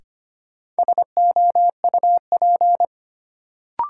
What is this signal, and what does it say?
Morse code, heard as a keyed tone, about 25 wpm, 700 Hz. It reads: SOUP
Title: Sentences from Top 1000 Words - Review Part 6 - New Word Drill - 25wpm